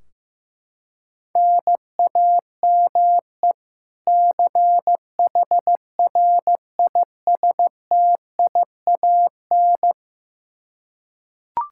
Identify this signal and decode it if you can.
Morse code, heard as a keyed tone, about 15 wpm, 700 Hz. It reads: NAME CHRISTIAN
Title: K1USN SST QSO Practice 15wpm